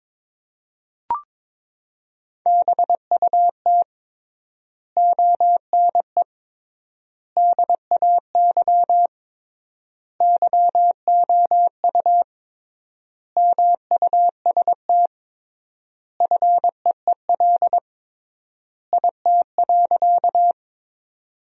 Morse code, heard as a keyed tone, about 22 wpm, 700 Hz. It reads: BUT ONE DAY YOU MUST FEEL IT.
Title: 3x Word-Spacing - Sentences from Top 400 Words - 22wpm